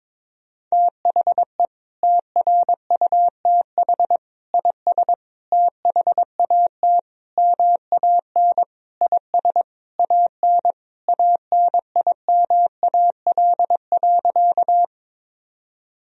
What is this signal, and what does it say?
Morse code, heard as a keyed tone, about 22 wpm, 700 Hz. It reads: THE TRUTH IS THAT MAN IS AN ANIMAL.
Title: Sentences from Top 1000 Words - Review Part 1 - 22wpm